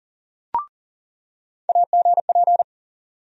Morse code, heard as a keyed tone, about 40 wpm, 700 Hz. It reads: AGP
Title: USA Call Sign Postfixes - 3 Letters 40wpm